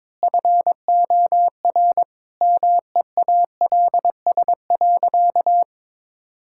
Morse code, heard as a keyed tone, about 22 wpm, 700 Hz. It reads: FOR MEALS.
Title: Pangram Challenge v2 - 22wpm